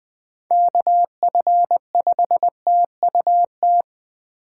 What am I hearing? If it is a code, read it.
Morse code, heard as a keyed tone, about 20 wpm, 700 Hz. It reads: KF5TUT
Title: USA Call Signs - 6 Characters - 2x Character Spacing - 20wpm